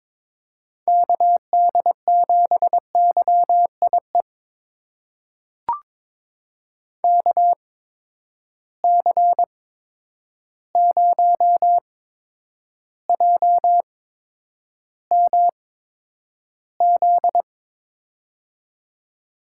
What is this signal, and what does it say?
Morse code, heard as a keyed tone, about 21 wpm, spaced as for 9 wpm, 700 Hz. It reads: KD7YIE KC0JMZ
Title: USA Call Signs - 6 Characters - 8x Character Spacing - 22wpm